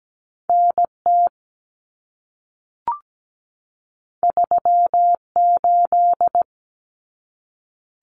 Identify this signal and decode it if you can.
Morse code, heard as a keyed tone, about 17 wpm, 700 Hz. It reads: NT 38
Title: Common Cut Numbers - 2-Digit 17wpm